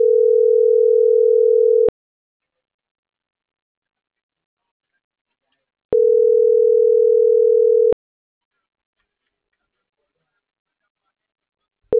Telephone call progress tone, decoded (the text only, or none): ringback tone